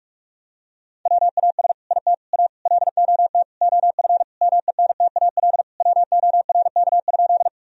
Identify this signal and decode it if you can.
Morse code, heard as a keyed tone, about 41 wpm, 700 Hz. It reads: WAS IT A LOT OF MENTAL WORK?
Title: Sentences from Top 1000 Words - Review Part 4 - 45wpm